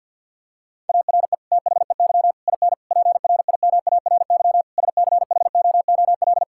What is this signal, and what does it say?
Morse code, heard as a keyed tone, about 50 wpm, 700 Hz. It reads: ARE THEY IN PRIMARY SCHOOL
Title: Sentences from Top 1000 Words - Review Part 5 - 50wpm